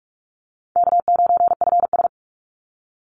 Morse code, heard as a keyed tone, about 44 wpm, 700 Hz. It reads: K9FS
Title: Easy Call Signs - 45wpm